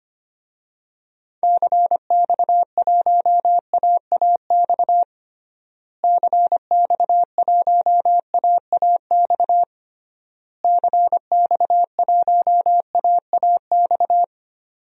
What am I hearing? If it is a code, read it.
Morse code, heard as a keyed tone, about 25 wpm, 700 Hz. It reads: CX1AAX CX1AAX CX1AAX
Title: DXCC-100 Call Signs - 6 Characters - Repeated 3x - 25wpm